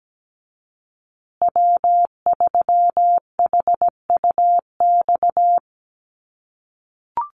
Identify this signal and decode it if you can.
Morse code, heard as a keyed tone, about 17 wpm, 700 Hz. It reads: W3HUX
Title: USA Call Signs Repeated 2x 17wpm